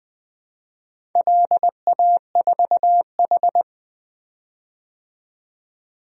Morse code, heard as a keyed tone, about 20 wpm, 700 Hz. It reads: LA4H
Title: DXCC-100 Call Signs - 4 Characters - 20wpm